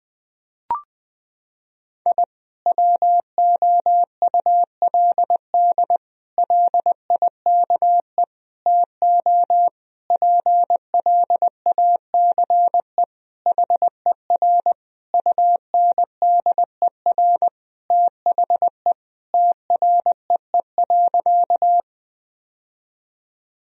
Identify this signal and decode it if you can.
Morse code, heard as a keyed tone, about 21 wpm, 700 Hz. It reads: I WOULD LIKE TO PLACE HER UNDER THE TREE.
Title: Sentences from Top 1000 Words - Review Part 1 - 20wpm